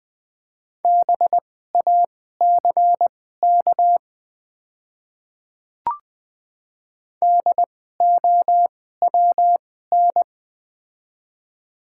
Morse code, heard as a keyed tone, about 20 wpm, spaced as for 15 wpm, 700 Hz. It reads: BACK DOWN
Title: 2x Character Spacing - Top 200 Words - 20wpm